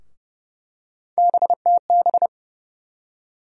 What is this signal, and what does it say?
Morse code, heard as a keyed tone, about 30 wpm, 700 Hz. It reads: BTB
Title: USA Call Sign Postfixes - 3 Letters 30wpm